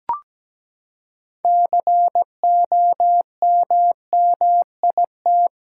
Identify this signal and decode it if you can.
Morse code, heard as a keyed tone, about 17 wpm, 700 Hz. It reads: COMMIT
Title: Six Letter Words - 17wpm